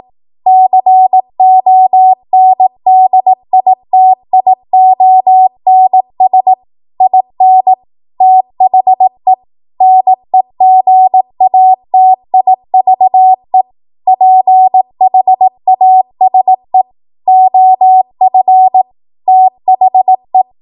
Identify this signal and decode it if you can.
Morse code, heard as a keyed tone, about 18 wpm, 750 Hz. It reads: CONDITIONS IN THE NEGATIVE PHASE OF THE
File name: Archive / 18 WPM / 18WPM